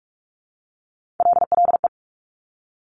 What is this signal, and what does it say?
Morse code, heard as a keyed tone, about 46 wpm, 700 Hz. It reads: LLE
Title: Top Three Letter Combinations in Words 45wpm